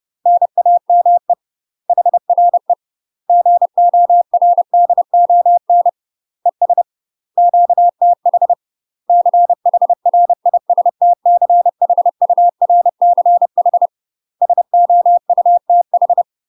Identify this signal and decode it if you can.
Morse code, heard as a keyed tone, about 30 wpm, 700 Hz. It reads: NAME HRE GORDON ES QTH CHRISTCHURCH SOUTH
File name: UK MASTER QSOs TRAINING / 30WPM / UK MASTER QSOs 0020